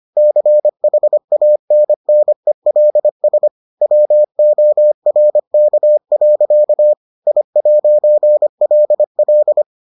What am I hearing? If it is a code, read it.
Morse code, heard as a keyed tone, about 25 wpm, 600 Hz. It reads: CHANNELS WORK. I'LL